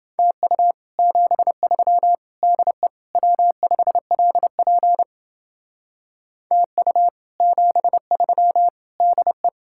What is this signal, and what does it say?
Morse code, heard as a keyed tone, about 29 wpm, 700 Hz. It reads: TU 73 DE W5LP TU 73 DE